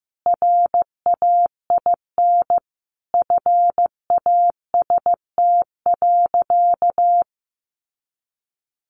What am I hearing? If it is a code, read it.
Morse code, heard as a keyed tone, about 15 wpm, 700 Hz. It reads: RAIN FAST.
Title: Sentences from Top 800 Words 15wpm